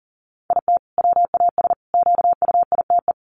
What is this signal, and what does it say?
Morse code, heard as a keyed tone, about 40 wpm, 700 Hz. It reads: IT WAS QUITE
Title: Sentences from Top 700 Words 40wpm